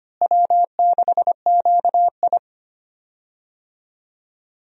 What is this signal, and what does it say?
Morse code, heard as a keyed tone, about 25 wpm, 700 Hz. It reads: W6QI
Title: USA Call Signs - 4 Characters - Repeated 3x - 25wpm